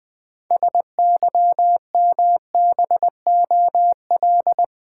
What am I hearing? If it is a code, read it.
Morse code, heard as a keyed tone, about 20 wpm, 700 Hz. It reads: SYMBOL
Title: Build A Word - 3 Letters At A Time - 8-Letter Words - 20wpm